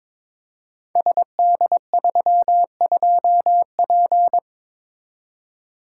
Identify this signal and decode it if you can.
Morse code, heard as a keyed tone, about 22 wpm, 700 Hz. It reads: SD32P
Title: Code Groups - 5 Letter:Numbers - 22wpm